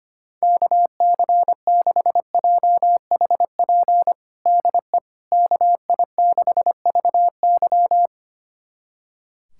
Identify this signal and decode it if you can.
Morse code, heard as a keyed tone, about 25 wpm, 700 Hz. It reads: KC6JHP DE KI6VY
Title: USA Callsign DE Callsign 25wpm